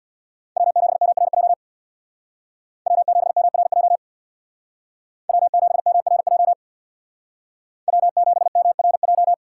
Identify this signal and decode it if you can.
Morse code, heard as a keyed tone, about 62 wpm, 700 Hz. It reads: W7MRJ W7MRJ W7MRJ W7MRJ
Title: Speed Racing - USA Call Signs 50wpm